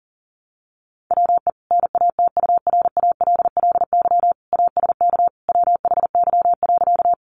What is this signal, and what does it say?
Morse code, heard as a keyed tone, about 40 wpm, 700 Hz. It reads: WE NATURALLY ASK WHY.
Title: Sentences from Top 1300 Words 40wpm